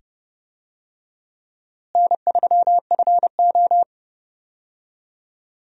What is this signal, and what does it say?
Morse code, heard as a keyed tone, about 30 wpm, 700 Hz. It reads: N3FO
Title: USA Call Signs - 4 Characters - Repeated 2x - 30wpm